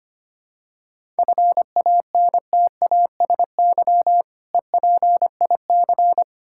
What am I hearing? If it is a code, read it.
Morse code, heard as a keyed tone, about 25 wpm, 700 Hz. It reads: FANTASY EPIC
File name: practice / 25wpm / Difficult Words with Hints - 25wpm